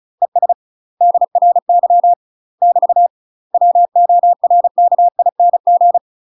Morse code, heard as a keyed tone, about 36 wpm, 700 Hz. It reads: ES DRY <BT> WORKING
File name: UK MASTER QSOs TRAINING / 35WPM / UK MASTER QSOs 0014